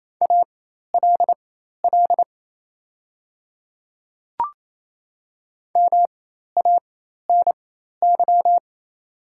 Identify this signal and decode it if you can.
Morse code, heard as a keyed tone, about 27 wpm, spaced as for 13 wpm, 700 Hz. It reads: ALL MANY
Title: 4x Character Spacing - Top 100 Words - 28wpm